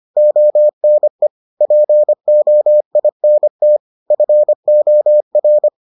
Morse code, heard as a keyed tone, about 25 wpm, 600 Hz. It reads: ONE POINT FOR